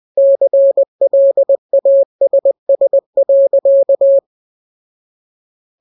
Morse code, heard as a keyed tone, about 20 wpm, 550 Hz. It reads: CLASS.
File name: jokes / joke 3 20 0000